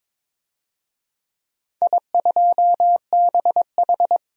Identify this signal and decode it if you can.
Morse code, heard as a keyed tone, about 23 wpm, 700 Hz. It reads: I2BH
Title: DXCC-100 Call Signs - 4 Characters - 22wpm